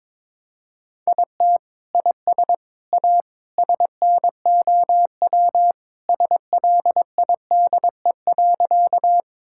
Morse code, heard as a keyed tone, about 22 wpm, 700 Hz. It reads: IT IS A SNOW SLIDE.
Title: Sentences from Top 1600 Words 22wpm